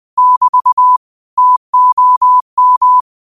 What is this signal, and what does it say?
Morse code, heard as a keyed tone, about 20 wpm, 1000 Hz. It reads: <BT> TOM